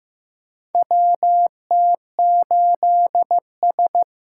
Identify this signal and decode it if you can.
Morse code, heard as a keyed tone, about 15 wpm, 700 Hz. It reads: WT8S
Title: USA Call Signs - 4 Characters - 8x Character Spacing - 15wpm